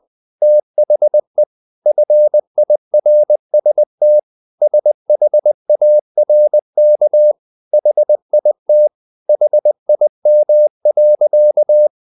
Morse code, heard as a keyed tone, about 20 wpm, 600 Hz. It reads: THE FIRST SHARK HIT HIM.
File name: the old man and the sea part 2 20wpm